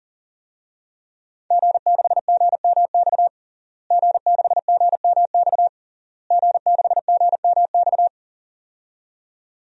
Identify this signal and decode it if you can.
Morse code, heard as a keyed tone, about 37 wpm, 700 Hz. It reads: G6GMX G6GMX G6GMX
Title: DXCC-100 Call Signs - 5 Characters - Repeated 3x - 40wpm